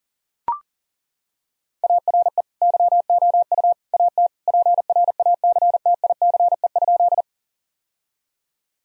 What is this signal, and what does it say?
Morse code, heard as a keyed tone, about 40 wpm, 700 Hz. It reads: ARE YOU AT PRACTICE?
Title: Sentences from Top 1000 Words - Review Part 4 - 40wpm